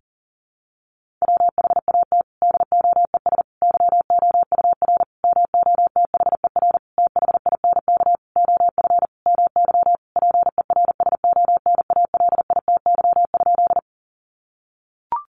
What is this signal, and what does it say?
Morse code, heard as a keyed tone, about 40 wpm, 700 Hz. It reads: WHAT DOES YOUR MOTHER THINK OF MY PERSONALITY?
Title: Sentences from Top 1400 Words 40wpm